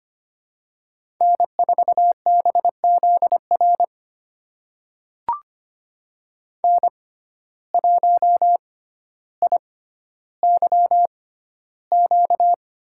Morse code, heard as a keyed tone, about 25 wpm, spaced as for 13 wpm, 700 Hz. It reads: N4BZR N1IYQ
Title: USA Call Signs - 5 Characters - 6x Character Spacing - 25wpm